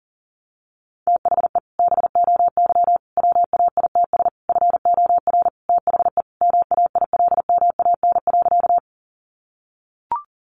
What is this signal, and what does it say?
Morse code, heard as a keyed tone, about 37 wpm, 700 Hz. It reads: THE BOY WAITS FOR THE MAILMAN.
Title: Additional Easy Phrases 40wpm